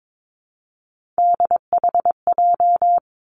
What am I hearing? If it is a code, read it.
Morse code, heard as a keyed tone, about 22 wpm, 700 Hz. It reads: DHJ
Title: Three Challenging Letter-Number Combinations 22wpm